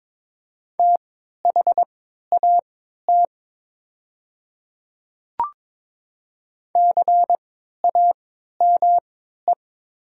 Morse code, heard as a keyed tone, about 23 wpm, spaced as for 13 wpm, 690 Hz. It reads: THAT CAME
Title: 3x Character Spacing - Top 200 Words - 22wpm